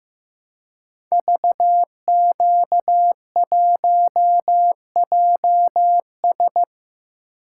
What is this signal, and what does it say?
Morse code, heard as a keyed tone, about 15 wpm, 700 Hz. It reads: VQ1JS
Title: Code Groups - 5 Letter:Numbers - 15wpm